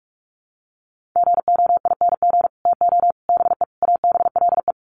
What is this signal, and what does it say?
Morse code, heard as a keyed tone, about 44 wpm, 700 Hz. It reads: GOING TO BE ABLE
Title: Sets of 4 Words 45wpm